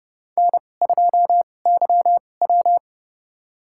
Morse code, heard as a keyed tone, about 30 wpm, spaced as for 22 wpm, 700 Hz. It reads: N2YW
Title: USA Call Signs - 4 Characters - 2x Character Spacing - 30wpm